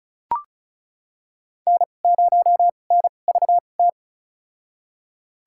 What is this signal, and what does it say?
Morse code, heard as a keyed tone, about 36 wpm, spaced as for 25 wpm, 700 Hz. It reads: N0NVT
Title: USA Call Signs - 5 Characters - 2x Character Spacing - 35wpm